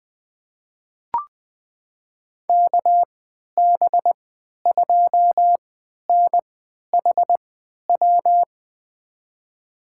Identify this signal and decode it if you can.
Morse code, heard as a keyed tone, about 20 wpm, spaced as for 11 wpm, 700 Hz. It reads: KB2NHW
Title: USA Call Signs - 6 Characters - 3x Character Spacing - 20wpm